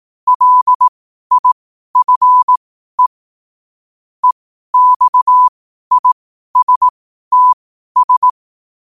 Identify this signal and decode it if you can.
Morse code, heard as a keyed tone, about 18 wpm, spaced as for 12 wpm, 1000 Hz. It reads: LIFE EXISTS
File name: qotdfiles / 12wpm